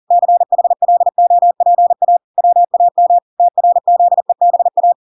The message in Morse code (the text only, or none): CHLOPA WAM TRZEBA